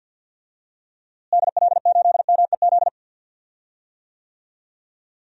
Morse code, heard as a keyed tone, about 50 wpm, 700 Hz. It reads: DL8MEZ
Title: DXCC-100 Call Signs - 6 Characters - Repeated 2x - 50wpm